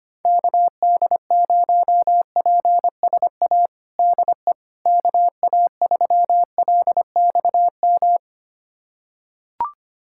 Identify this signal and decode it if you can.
Morse code, heard as a keyed tone, about 25 wpm, 700 Hz. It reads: KD0PSA DE KA3LXM
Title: USA Callsign DE Callsign - 6 Characters - 25wpm